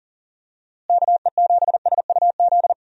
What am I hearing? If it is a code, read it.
Morse code, heard as a keyed tone, about 37 wpm, 700 Hz. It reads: KE7SUZ
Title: USA Call Signs - 6 Characters - Repeated 2x - 40wpm